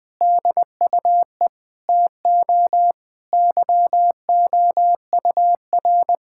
Morse code, heard as a keyed tone, about 21 wpm, 700 Hz. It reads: DUE TO YOUR
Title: Sentences from Top 1000 Words - Review Part 6 - 20wpm